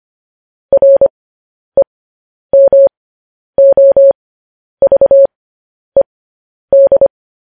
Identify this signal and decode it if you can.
Morse code, heard as a keyed tone, about 25 wpm, spaced as for 10 wpm, 550 Hz. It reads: REMOVED
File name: podcast / News headlines-1x-25x10